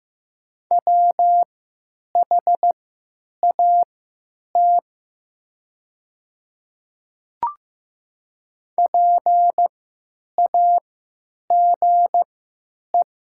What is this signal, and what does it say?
Morse code, heard as a keyed tone, about 15 wpm, spaced as for 9 wpm, 700 Hz. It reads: WHAT PAGE
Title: 3x Character Spacing - Top 300 Words - 15wpm